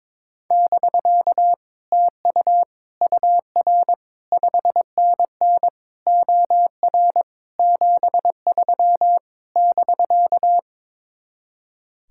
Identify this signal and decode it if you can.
Morse code, heard as a keyed tone, about 22 wpm, 700 Hz. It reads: <BK> TU UR 5NN OR 73 <BK>
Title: Speed-Racing Common QSO Phrases - 22wpm